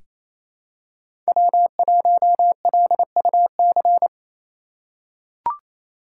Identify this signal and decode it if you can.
Morse code, heard as a keyed tone, about 28 wpm, 700 Hz. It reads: W1LUC
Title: USA Call Signs - 4x Character Spacing - 28wpm